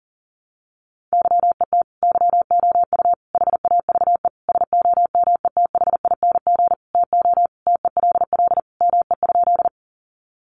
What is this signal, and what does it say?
Morse code, heard as a keyed tone, about 40 wpm, 700 Hz. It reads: YET YOU HAVE SOMETHING TO TELL ME?
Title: Sentences from Top 600 Words 40wpm